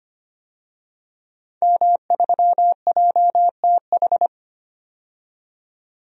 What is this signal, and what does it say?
Morse code, heard as a keyed tone, about 25 wpm, 700 Hz. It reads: M3JTH
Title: DXCC-100 Call Signs - 5 Characters - 25wpm